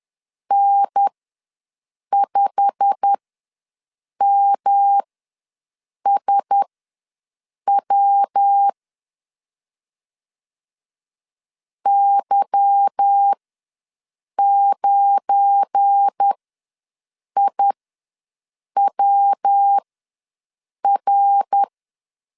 Morse code, heard as a keyed tone, about 10 wpm, spaced as for 6 wpm, 800 Hz. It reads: N5MSW Y9IWR
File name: LEZIONI / 8 YJ45 / 8 YJ45 ws 10wpm 3843